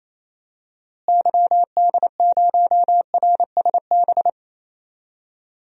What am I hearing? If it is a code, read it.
Morse code, heard as a keyed tone, about 28 wpm, 700 Hz. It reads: YD0RSB
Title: DXCC-100 Call Signs - 6 Characters - Repeated 3x - 28wpm